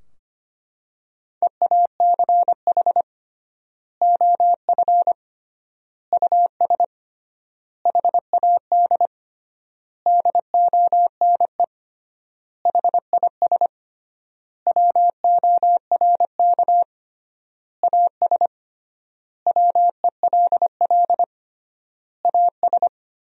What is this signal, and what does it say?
Morse code, heard as a keyed tone, about 25 wpm, 700 Hz. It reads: EACH OF US HAD DONE HIS WORK AS WELL AS
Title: 3x Word-Spacing - Sentences from Top 500 Words - 25wpm